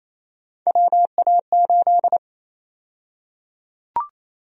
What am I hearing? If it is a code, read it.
Morse code, heard as a keyed tone, about 28 wpm, 700 Hz. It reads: WA8
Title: USA Call Sign Prefixes - 28wpm